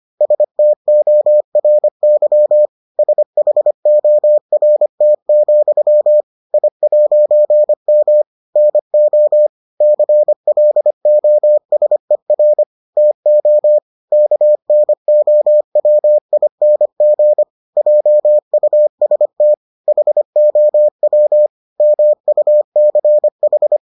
Morse code, heard as a keyed tone, about 25 wpm, 600 Hz. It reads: STORY SHORT, I'M NO CLOSER TO KNOWING JUST HOW MUCH